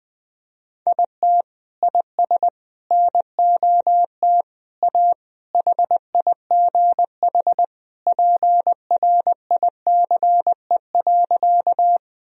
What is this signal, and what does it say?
Morse code, heard as a keyed tone, about 20 wpm, 700 Hz. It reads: IT IS NOT A HIGH PRICE.
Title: Sentences from Top 1000 Words - Review Part 4 - 20wpm